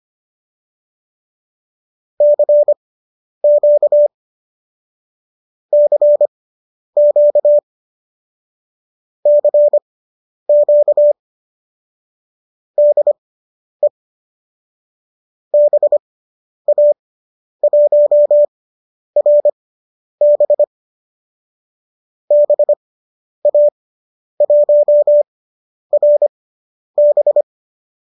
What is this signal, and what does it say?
Morse code, heard as a keyed tone, about 25 wpm, spaced as for 10 wpm, 590 Hz. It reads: CQ CQ CQ DE BA1RB BA1RB